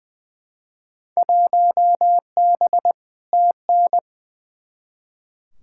Morse code, heard as a keyed tone, about 21 wpm, 700 Hz. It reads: 1B TN
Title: Field Day Exchanges - 20wpm